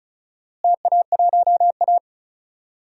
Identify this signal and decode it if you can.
Morse code, heard as a keyed tone, about 34 wpm, 700 Hz. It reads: TA1A
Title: DXCC-100 Call Signs - 4 Characters - Repeated 3x - 35wpm